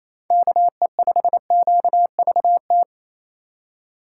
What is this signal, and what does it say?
Morse code, heard as a keyed tone, about 28 wpm, 700 Hz. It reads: KE5QVT
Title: USA Call Signs - 6 Characters - 2x Character Spacing - 28wpm